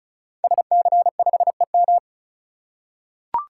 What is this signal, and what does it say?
Morse code, heard as a keyed tone, about 36 wpm, 690 Hz. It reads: SC5EM
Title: Code Groups - 5 Letter:Numbers - 35wpm